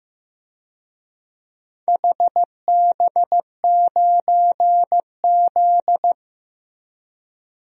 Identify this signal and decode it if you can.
Morse code, heard as a keyed tone, about 15 wpm, 700 Hz. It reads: HB9Z
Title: DXCC-100 Call Signs - 4 Characters - Repeated 3x - 15wpm